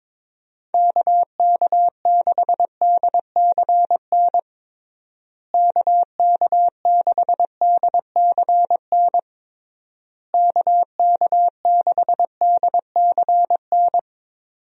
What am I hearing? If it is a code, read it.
Morse code, heard as a keyed tone, about 22 wpm, 700 Hz. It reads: KK6DCN KK6DCN KK6DCN
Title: USA Call Signs - 6 Characters - Repeated 3x - 22wpm